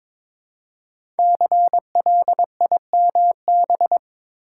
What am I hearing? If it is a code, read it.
Morse code, heard as a keyed tone, about 22 wpm, 700 Hz. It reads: CLIMB
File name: 5 Letter Words - First 3 Letters Given - 22wpm